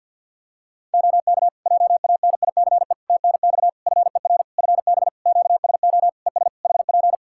Morse code, heard as a keyed tone, about 50 wpm, 700 Hz. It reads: OK JANICE TNX FER FB QSO ES HP
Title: K1USN SST QSO Practice 50wpm